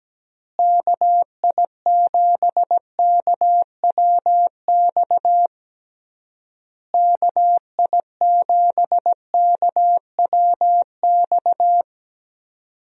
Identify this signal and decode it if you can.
Morse code, heard as a keyed tone, about 17 wpm, 700 Hz. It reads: KI7KWX KI7KWX
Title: USA Call Signs - 6 Characters - Repeated 3x - 17wpm